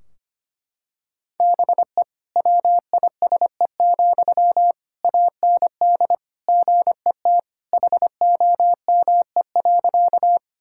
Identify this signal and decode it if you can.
Morse code, heard as a keyed tone, about 25 wpm, 700 Hz. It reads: BE WISE, AND GET HOME.
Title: Sentences from Top 1600 Words 25wpm